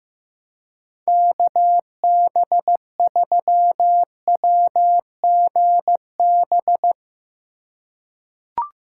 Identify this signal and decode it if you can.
Morse code, heard as a keyed tone, about 15 wpm, 700 Hz. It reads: KB3WGB
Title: USA Call Signs - 6 Characters - 3x Character Spacing - 15wpm